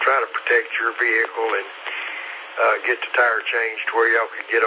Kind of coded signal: Emergency Alert System (SAME) tones